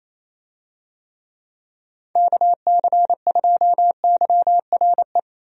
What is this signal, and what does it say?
Morse code, heard as a keyed tone, about 27 wpm, 700 Hz. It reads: KC2YRE
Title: USA Call Signs - 6 Characters - 28wpm